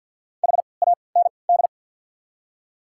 Morse code, heard as a keyed tone, about 53 wpm, spaced as for 29 wpm, 690 Hz. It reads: HAND
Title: 3x Character Spacing - Top 200 Words - 50wpm